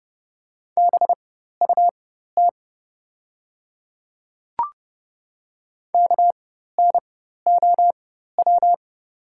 Morse code, heard as a keyed tone, about 30 wpm, spaced as for 14 wpm, 700 Hz. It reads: BUT KNOW
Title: 4x Character Spacing - Top 500 Words - 30wpm